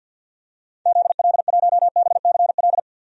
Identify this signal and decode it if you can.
Morse code, heard as a keyed tone, about 46 wpm, 700 Hz. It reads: ZL1BCL
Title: DXCC-100 Call Signs - 6 Characters - 50wpm